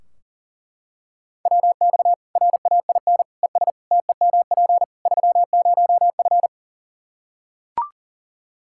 Morse code, heard as a keyed tone, about 40 wpm, 700 Hz. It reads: WX RAIN ES TEMP 30F
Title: K1USN SST QSO Practice 40wpm